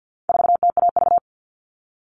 Morse code, heard as a keyed tone, about 50 wpm, 700 Hz. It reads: 4TAV
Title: All Cut Numbers - 4-Digit 50wpm